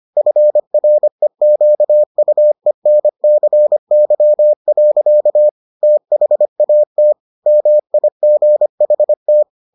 Morse code, heard as a keyed tone, about 25 wpm, 600 Hz. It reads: FREQUENCY. THAT MIGHT